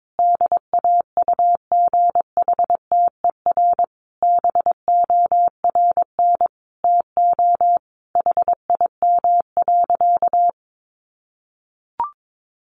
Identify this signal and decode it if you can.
Morse code, heard as a keyed tone, about 22 wpm, 700 Hz. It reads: DAUGHTER BORN TO HIM.
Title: Sentences from Top 1500 Words 22wpm